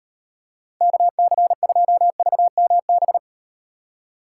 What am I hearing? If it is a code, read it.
Morse code, heard as a keyed tone, about 38 wpm, 700 Hz. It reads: KC2VMB